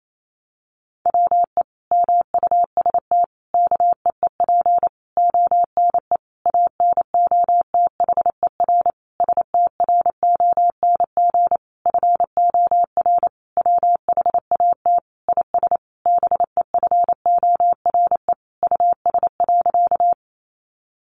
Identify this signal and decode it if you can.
Morse code, heard as a keyed tone, about 28 wpm, 700 Hz. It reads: WE MUST KEEP ONE ANOTHER STRONG FOR WHAT IS BEFORE US.
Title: Sentences from Top 600 Words 28wpm